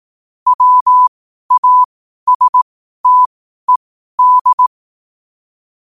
Morse code, heard as a keyed tone, about 18 wpm, spaced as for 12 wpm, 1000 Hz. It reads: WASTED
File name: qotdfiles / 12wpm